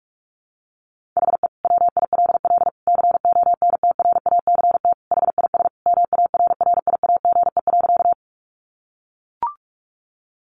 Morse code, heard as a keyed tone, about 44 wpm, 700 Hz. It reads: HE WILL CONTRACT HIS MARRIAGE.
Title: Sentences from Top 1400 Words 45wpm